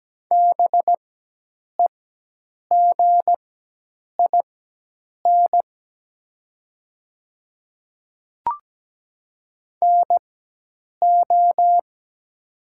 Morse code, heard as a keyed tone, about 17 wpm, spaced as for 8 wpm, 700 Hz. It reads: BEGIN NO
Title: 4x Character Spacing - Top 300 Words - 17wpm